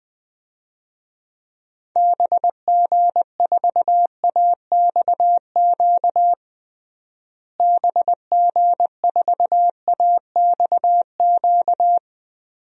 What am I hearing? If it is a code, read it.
Morse code, heard as a keyed tone, about 20 wpm, 700 Hz. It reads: BG4AXQ BG4AXQ
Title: DXCC-100 Call Signs - 6 Characters - Repeated 2x - 20wpm